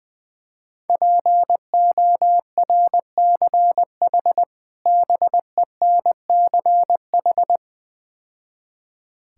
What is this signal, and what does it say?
Morse code, heard as a keyed tone, about 20 wpm, 700 Hz. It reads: PORCH BENCH
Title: Difficult Words - Speed-Racing with Hints - 20wpm